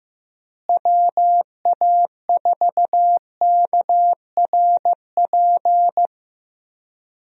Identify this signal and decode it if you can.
Morse code, heard as a keyed tone, about 15 wpm, 700 Hz. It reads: WA4KRP
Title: USA Call Signs - 6 Characters - Repeated 3x - 15wpm